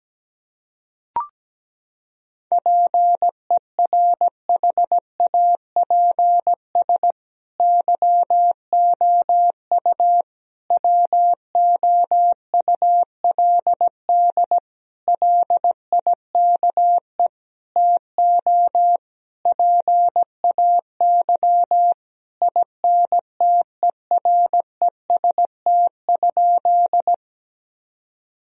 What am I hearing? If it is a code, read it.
Morse code, heard as a keyed tone, about 17 wpm, 700 Hz. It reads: PERHAPS YOU WOULD LIKE TO PAY INTEREST?
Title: Sentences from Top 1000 Words - Review Part 2 - 17wpm